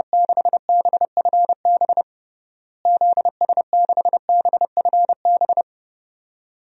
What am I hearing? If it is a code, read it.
Morse code, heard as a keyed tone, about 32 wpm, 700 Hz. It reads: 6BFB ZS6BFB
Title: DXCC-100 Call Signs - 6 Characters - Repeated 2x - 30wpm